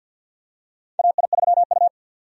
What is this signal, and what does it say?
Morse code, heard as a keyed tone, about 47 wpm, 700 Hz. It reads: AI3U
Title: USA Call Signs - 4 Characters - 50wpm